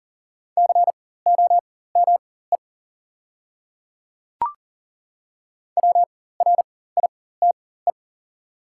Morse code, heard as a keyed tone, about 40 wpm, spaced as for 19 wpm, 700 Hz. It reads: COME WRITE
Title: 4x Character Spacing - Top 300 Words - 40wpm